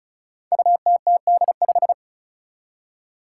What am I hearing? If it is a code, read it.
Morse code, heard as a keyed tone, about 36 wpm, 700 Hz. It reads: UTTD5
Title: Five Challenging Letter-Number Combinations 35wpm